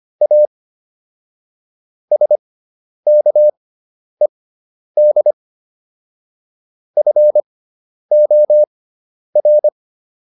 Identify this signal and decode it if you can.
Morse code, heard as a keyed tone, about 25 wpm, spaced as for 10 wpm, 600 Hz. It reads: A SKED FOR